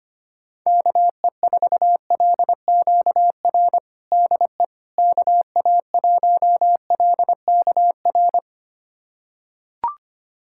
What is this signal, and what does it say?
Morse code, heard as a keyed tone, about 25 wpm, 700 Hz. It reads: KE4LQR DE KA1LKR
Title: USA Callsign DE Callsign - 6 Characters - 25wpm